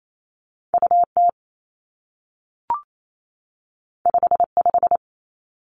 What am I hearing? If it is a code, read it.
Morse code, heard as a keyed tone, about 28 wpm, 700 Hz. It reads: U T 5 5
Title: Two Challenging Letter-Number Combinations 28wpm